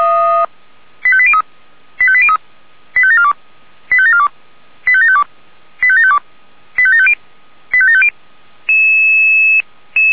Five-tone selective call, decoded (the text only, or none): 85803 85803 85742 85742 85752 85752 85750 85750